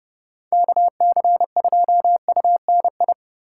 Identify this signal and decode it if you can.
Morse code, heard as a keyed tone, about 34 wpm, 700 Hz. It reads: KC2UNI